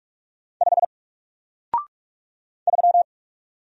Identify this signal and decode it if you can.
Morse code, heard as a keyed tone, about 46 wpm, 700 Hz. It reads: 5 3
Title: Single Letter-Number Alt 45wpm